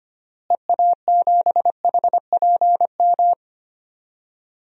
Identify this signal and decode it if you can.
Morse code, heard as a keyed tone, about 25 wpm, 700 Hz. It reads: EA7HPM
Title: DXCC-100 Call Signs - 6 Characters - Repeated 3x - 25wpm